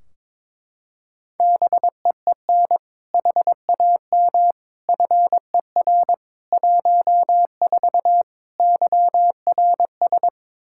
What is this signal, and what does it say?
Morse code, heard as a keyed tone, about 22 wpm, 700 Hz. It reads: BEEN HAM FER 14 YRS
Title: Practice QSO - Exchange 3 - More Info - 22wpm